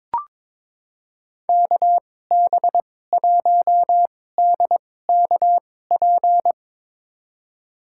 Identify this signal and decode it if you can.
Morse code, heard as a keyed tone, about 21 wpm, spaced as for 16 wpm, 700 Hz. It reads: KB1DKP
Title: USA Call Signs - 6 Characters - 2x Character Spacing - 22wpm